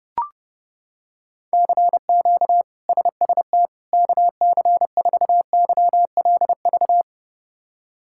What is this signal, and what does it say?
Morse code, heard as a keyed tone, about 32 wpm, 700 Hz. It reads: CQ SST KC4YLV
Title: K1USN SST QSO Practice - 30wpm